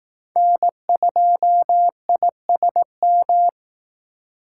Morse code, heard as a keyed tone, about 18 wpm, 700 Hz. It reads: N2ISM
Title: Speed Racing - USA Call Signs 15wpm